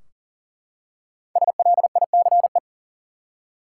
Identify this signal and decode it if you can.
Morse code, heard as a keyed tone, about 40 wpm, 700 Hz. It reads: SLICE
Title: Five Letter Words - 40wpm